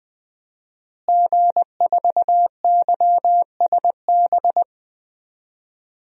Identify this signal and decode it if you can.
Morse code, heard as a keyed tone, about 20 wpm, 700 Hz. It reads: G4YSB
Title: DXCC-100 Call Signs - 5 Characters - Repeated 3x - 20wpm